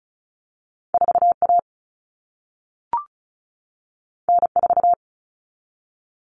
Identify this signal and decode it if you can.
Morse code, heard as a keyed tone, about 36 wpm, 700 Hz. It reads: 4A N4
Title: All Cut Numbers - 2-Digit 35wpm